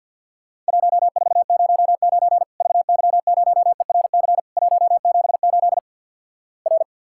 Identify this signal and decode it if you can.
Morse code, heard as a keyed tone, about 50 wpm, 700 Hz. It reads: 1409 VY0ERC 178 R
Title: Worked All Europe DX Contest - QTC Exchange with Repeats - 50wpm